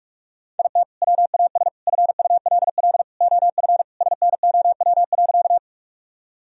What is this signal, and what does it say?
Morse code, heard as a keyed tone, about 41 wpm, 700 Hz. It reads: IT WAS FULL OF SNOW.
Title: Sentences from Top 1000 Words - Review Part 3 - 45wpm